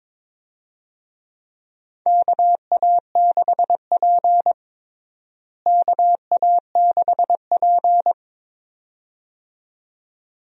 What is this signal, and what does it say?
Morse code, heard as a keyed tone, about 22 wpm, 700 Hz. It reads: KA6P KA6P
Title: USA Call Signs - 4 Characters - Repeated 2x - 22wpm